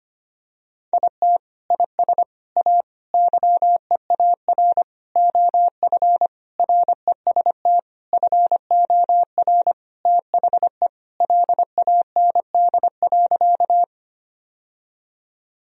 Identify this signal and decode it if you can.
Morse code, heard as a keyed tone, about 25 wpm, 700 Hz. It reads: IT IS A YEAR OF REST FOR THE LAND.
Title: Sentences from Top 1000 Words - Review Part 1 - 25wpm